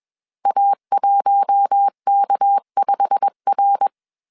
Morse code, heard as a keyed tone, about 21 wpm, 800 Hz. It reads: A1K5R
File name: CW / DET / 1 9 1500C S C 20wpm 2310